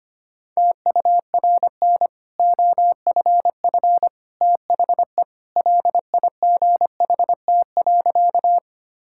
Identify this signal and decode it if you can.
Morse code, heard as a keyed tone, about 25 wpm, 690 Hz. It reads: TURN OFF THE LIGHT.
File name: Easy Sentences with Hints - 25wpm